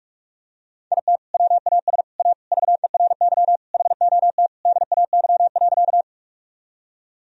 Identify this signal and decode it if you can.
Morse code, heard as a keyed tone, about 47 wpm, 700 Hz. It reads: IT WAS A VERY HOT DAY.
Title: Sentences from Top 1000 Words - Review Part 3 - 45wpm